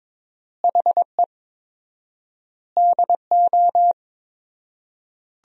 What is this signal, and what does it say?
Morse code, heard as a keyed tone, about 22 wpm, 700 Hz. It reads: HE DO
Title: 4x Word-Spacing - Sentences from Top 100 Words - 22wpm